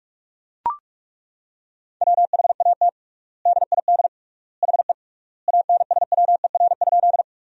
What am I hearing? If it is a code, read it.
Morse code, heard as a keyed tone, about 43 wpm, 700 Hz. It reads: WHAT DID HE ANSWER?
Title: 3x Word-Spacing - Sentences from Top 300 Words - 45wpm